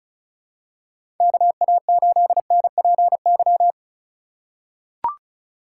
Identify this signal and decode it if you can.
Morse code, heard as a keyed tone, about 34 wpm, 700 Hz. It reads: KA8NPY
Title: USA Call Signs - 6 Characters - 2x Character Spacing - 35wpm